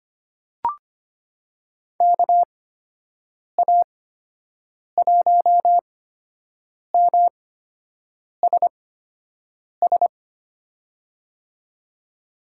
Morse code, heard as a keyed tone, about 25 wpm, spaced as for 7 wpm, 700 Hz. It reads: KA1MSS
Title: USA Call Signs - 8x Character Spacing - 25wpm